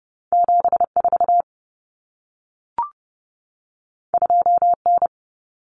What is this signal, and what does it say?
Morse code, heard as a keyed tone, about 30 wpm, 700 Hz. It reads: 74 2N
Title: Common Cut Numbers - 2-Digit 30wpm